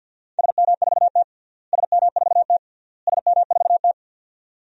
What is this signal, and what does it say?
Morse code, heard as a keyed tone, about 47 wpm, 700 Hz. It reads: SM4T SM4T SM4T
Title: DXCC-100 Call Signs - 4 Characters - Repeated 3x - 50wpm